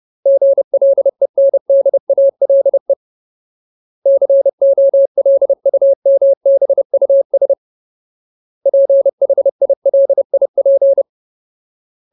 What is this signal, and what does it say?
Morse code, heard as a keyed tone, about 30 wpm, 550 Hz. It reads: GLENDALE COLUMBUS PHILIP
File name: Ultimate Word List / 30WPM Ultimate